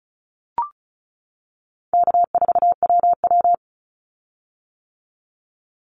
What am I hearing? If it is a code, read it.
Morse code, heard as a keyed tone, about 34 wpm, 700 Hz. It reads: K4WW
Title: Call Signs - 35wpm